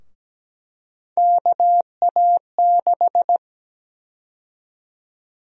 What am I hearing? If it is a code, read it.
Morse code, heard as a keyed tone, about 17 wpm, 700 Hz. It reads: KA6
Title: USA Call Sign Prefixes - Repeated 2x - 17wpm